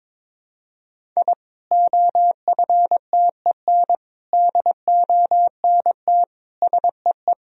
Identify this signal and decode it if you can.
Morse code, heard as a keyed tone, about 22 wpm, 700 Hz. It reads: I OFTEN DONT SEE
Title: Sentences from Top 1000 Words - Review Part 1 - 22wpm